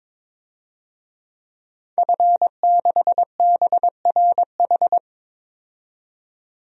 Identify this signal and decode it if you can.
Morse code, heard as a keyed tone, about 22 wpm, 700 Hz. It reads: F6BRH
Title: DXCC-100 Call Signs - 5 Characters - 22wpm